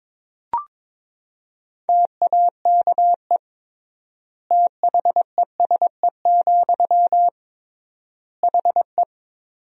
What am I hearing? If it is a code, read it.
Morse code, heard as a keyed tone, about 22 wpm, 700 Hz. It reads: TAKE THESE, HE
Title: 3x Word-Spacing - Sentences from Top 300 Words - 22wpm